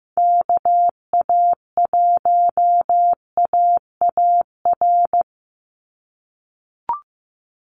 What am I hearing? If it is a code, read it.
Morse code, heard as a keyed tone, about 15 wpm, 700 Hz. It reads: KA1AAR
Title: USA Call Signs 15wpm